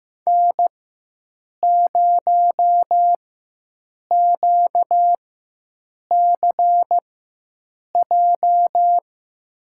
Morse code, heard as a keyed tone, about 15 wpm, spaced as for 7 wpm, 700 Hz. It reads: N0QCJ
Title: USA Call Signs - 5 Characters - 4x Character Spacing - 15wpm